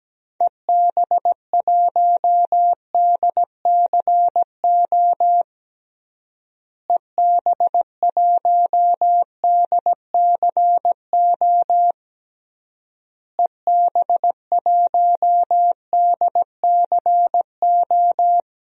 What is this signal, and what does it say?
Morse code, heard as a keyed tone, about 17 wpm, 700 Hz. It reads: EB1DCO EB1DCO EB1DCO
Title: DXCC-100 Call Signs - 6 Characters - Repeated 3x - 17wpm